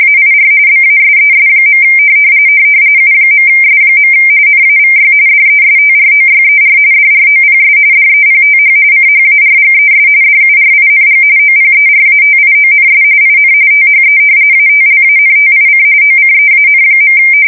RTTY, about 45.45 baud. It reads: AIR BASES. U.S. NAVY DESTROYER DROPS DEPTH CHARGES DURING OPERATIONS IN THE ATLANTIC OCEAN.
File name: fsk170 / usnavy4